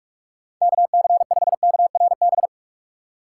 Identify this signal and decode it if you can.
Morse code, heard as a keyed tone, about 46 wpm, 700 Hz. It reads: KC5KRB
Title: USA Call Signs - 6 Characters - 4x Character Spacing - 45wpm